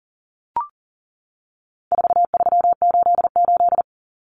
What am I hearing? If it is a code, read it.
Morse code, heard as a keyed tone, about 40 wpm, 700 Hz. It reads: 4388
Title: Common Cut Numbers - 4-Digit 40wpm